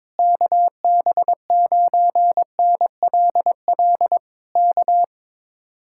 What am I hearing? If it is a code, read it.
Morse code, heard as a keyed tone, about 21 wpm, 700 Hz. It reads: KB9NLL K
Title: Speed-Racing Common QSO Phrases - 22wpm